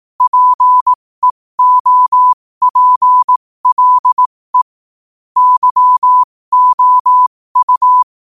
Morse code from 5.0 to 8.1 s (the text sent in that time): YOU